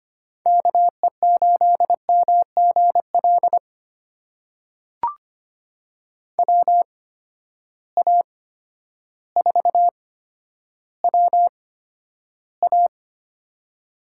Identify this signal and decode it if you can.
Morse code, heard as a keyed tone, about 25 wpm, spaced as for 11 wpm, 700 Hz. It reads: KE8MGL WA4WA
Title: USA Call Signs - 8x Character Spacing - 25wpm